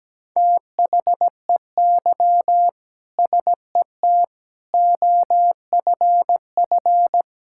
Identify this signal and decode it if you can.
Morse code, heard as a keyed tone, about 17 wpm, 700 Hz. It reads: THEY SET OFF
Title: Sentences from Top 1000 Words - Review Part 5 - 17wpm